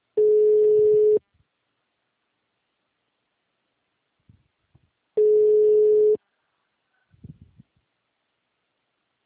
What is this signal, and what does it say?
Ringback tone (Europe, 425 Hz).
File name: out- -2011- - - .29